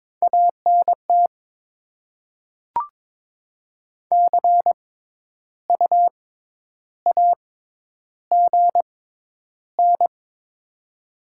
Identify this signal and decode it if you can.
Morse code, heard as a keyed tone, about 22 wpm, spaced as for 10 wpm, 690 Hz. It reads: ANT CU AGN
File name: Top 20 QSO Elements - 6x Character Spacing - 22wpm